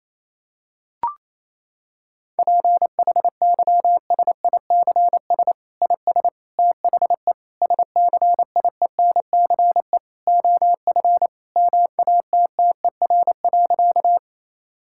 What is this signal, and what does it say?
Morse code, heard as a keyed tone, about 28 wpm, 700 Hz. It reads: PHYSICS IS THE SCIENCE OF MATTER.
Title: Sentences from Top 1000 Words - Review Part 5 - 28wpm